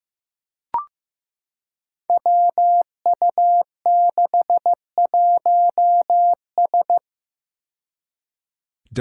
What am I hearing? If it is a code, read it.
Morse code, heard as a keyed tone, about 15 wpm, 700 Hz. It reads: WU61S
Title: Five Challenging Letter-Number Combinations 15wpm